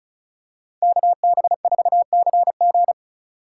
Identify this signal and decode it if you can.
Morse code, heard as a keyed tone, about 36 wpm, 700 Hz. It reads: KB4CG
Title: K1USN SST QSO Practice - 35wpm